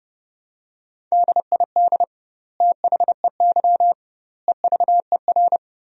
Morse code, heard as a keyed tone, about 32 wpm, 700 Hz. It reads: DID THEY EVER
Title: 2x Word-Spacing - Sentences from Top 400 Words - 30wpm